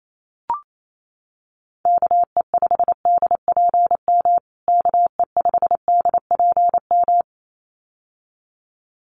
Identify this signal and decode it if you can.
Morse code, heard as a keyed tone, about 28 wpm, 700 Hz. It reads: KE5DPM KE5DPM
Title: USA Call Signs Repeated 2x 28wpm